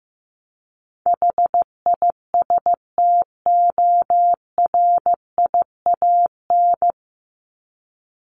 Morse code, heard as a keyed tone, about 15 wpm, 700 Hz. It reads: HISTORIAN
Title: Top 1751-2000 Words 15wpm